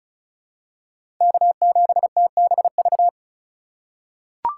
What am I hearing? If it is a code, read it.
Morse code, heard as a keyed tone, about 36 wpm, 700 Hz. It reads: K7TBV
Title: USA Call Signs - 5 Characters - 3x Character Spacing - 35wpm